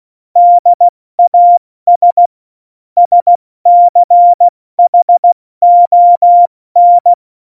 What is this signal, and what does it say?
Morse code, heard as a keyed tone, about 16 wpm, 700 Hz. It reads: DAS SCHON
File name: Morsekurs / Witze / Witz 15. September 2020, 70 F80